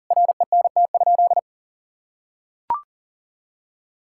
Morse code, heard as a keyed tone, about 40 wpm, 700 Hz. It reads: RENT?
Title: Sentences from Top 1000 Words - Review Part 5 - 40wpm